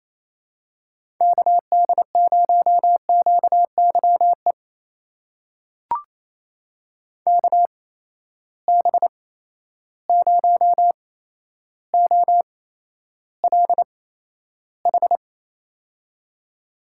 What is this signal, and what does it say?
Morse code, heard as a keyed tone, about 27 wpm, spaced as for 12 wpm, 700 Hz. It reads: KD0QYE KB0OLH
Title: USA Call Signs - 6 Characters - 8x Character Spacing - 28wpm